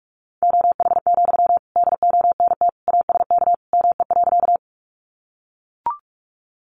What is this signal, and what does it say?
Morse code, heard as a keyed tone, about 44 wpm, 700 Hz. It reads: OH, DONT ASK ME.
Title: Sentences from Top 500 Words 45wpm